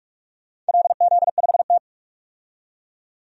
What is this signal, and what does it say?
Morse code, heard as a keyed tone, about 47 wpm, 700 Hz. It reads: LZ5T
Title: DXCC-100 Call Signs - 4 Characters - 45wpm